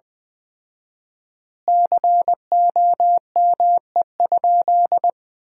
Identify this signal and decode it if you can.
Morse code, heard as a keyed tone, about 20 wpm, 700 Hz. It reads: COME?
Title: 4x Word-Spacing - Sentences from Top 200 Words - 20wpm